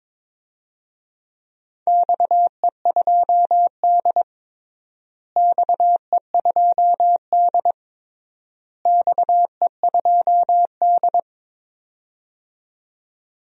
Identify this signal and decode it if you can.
Morse code, heard as a keyed tone, about 22 wpm, 700 Hz. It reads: XE2D XE2D XE2D
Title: DXCC-100 Call Signs - 4 Characters - Repeated 3x - 22wpm